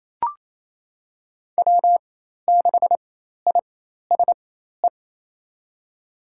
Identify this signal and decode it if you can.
Morse code, heard as a keyed tone, about 28 wpm, spaced as for 13 wpm, 700 Hz. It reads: W6ISE
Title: USA Call Signs - 5 Characters - 4x Character Spacing - 28wpm